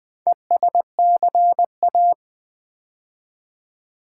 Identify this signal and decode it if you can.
Morse code, heard as a keyed tone, about 20 wpm, 700 Hz. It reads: ESCA
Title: Build A Word - 2 Letters At A Time - 6-Letter Words - 20wpm